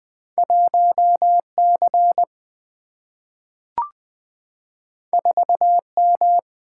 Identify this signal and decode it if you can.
Morse code, heard as a keyed tone, about 20 wpm, 700 Hz. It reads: 1 C 4 M
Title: Single-Letter Single-Number 20wpm